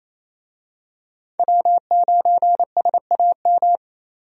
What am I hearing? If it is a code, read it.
Morse code, heard as a keyed tone, about 28 wpm, 700 Hz. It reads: W9SAM
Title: USA Call Signs - 5 Characters - 28wpm